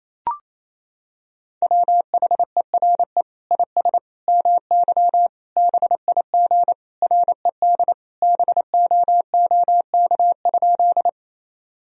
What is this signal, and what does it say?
Morse code, heard as a keyed tone, about 28 wpm, 700 Hz. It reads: WHERE IS MY BIG RED BOOK?
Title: Sentences from Top 1000 Words - Review Part 2 - 28wpm